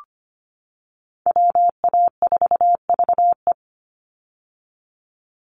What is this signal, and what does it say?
Morse code, heard as a keyed tone, about 25 wpm, 700 Hz. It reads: WA4VE
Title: USA Call Signs 25wpm